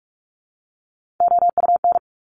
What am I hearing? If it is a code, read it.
Morse code, heard as a keyed tone, about 46 wpm, 700 Hz. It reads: OUN
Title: Top Three Letter Combinations in Words 45wpm